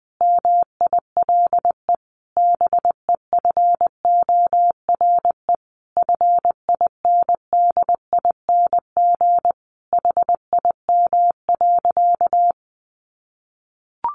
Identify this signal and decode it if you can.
Morse code, heard as a keyed tone, about 20 wpm, 700 Hz. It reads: MILE BEFORE FINDING HIM.
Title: Sentences from Top 1100 Words 20wpm